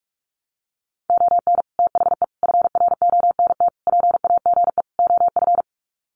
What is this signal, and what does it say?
Morse code, heard as a keyed tone, about 44 wpm, 700 Hz. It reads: ON THE FRONT PAGE OF
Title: Sets of 5 Words Encore 45wpm